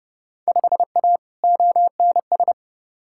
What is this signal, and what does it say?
Morse code, heard as a keyed tone, about 31 wpm, 700 Hz. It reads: 5A ONS
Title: Field Day Exchanges Encore - 30wpm